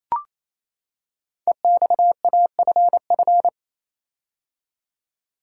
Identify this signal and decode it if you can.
Morse code, heard as a keyed tone, about 28 wpm, 700 Hz. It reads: EXAFF
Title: Code Groups - 5 Letters - 28wpm